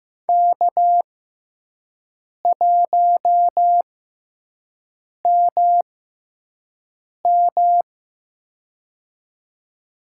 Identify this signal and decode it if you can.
Morse code, heard as a keyed tone, about 15 wpm, spaced as for 5 wpm, 700 Hz. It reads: K1MM
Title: USA Call Signs - 4 Characters - 6x Character Spacing - 15wpm